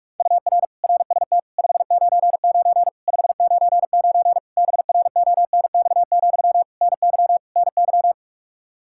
Morse code, heard as a keyed tone, about 46 wpm, 700 Hz. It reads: UR RST 599 599 BRONX, NY NY
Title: Practice QSO - Exchange 1 - Beginning - 45wpm